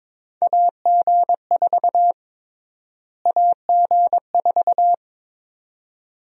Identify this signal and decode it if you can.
Morse code, heard as a keyed tone, about 21 wpm, 700 Hz. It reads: AG4 AG4
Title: USA Call Sign Prefixes - Repeated 2x - 22wpm